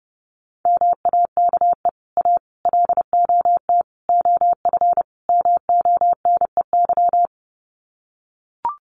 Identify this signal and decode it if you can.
Morse code, heard as a keyed tone, about 30 wpm, 700 Hz. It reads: MAKE A LOT OF MONEY
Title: Sets of 5 Words Encore2 30wpm